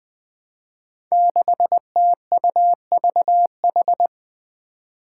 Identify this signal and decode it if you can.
Morse code, heard as a keyed tone, about 20 wpm, 700 Hz. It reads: 6TUVH
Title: Five Challenging Letter-Number Combinations 20wpm